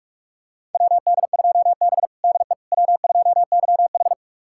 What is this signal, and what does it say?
Morse code, heard as a keyed tone, about 46 wpm, 700 Hz. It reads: WD2B DE W2YH
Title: USA Callsign DE Callsign - 4 Characters - 45wpm